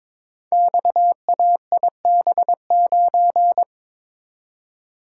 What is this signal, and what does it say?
Morse code, heard as a keyed tone, about 22 wpm, 690 Hz. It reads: XAIB9
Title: Code Groups - 5 Letter:Numbers - 22wpm